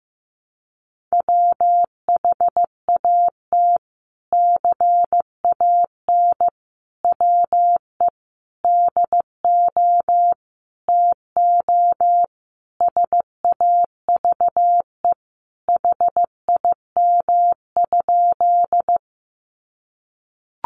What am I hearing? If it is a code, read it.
Morse code, heard as a keyed tone, about 15 wpm, 700 Hz. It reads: WHAT CAN WE DO TO SAVE HIM?
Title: Sentences from Top 1100 Words 15wpm